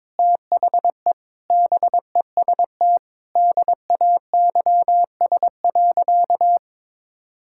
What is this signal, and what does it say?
Morse code, heard as a keyed tone, about 22 wpm, 700 Hz. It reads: THE BEST DAYS.
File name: practice / 22wpm / Sentences with 4-Word Phrases - 22wpm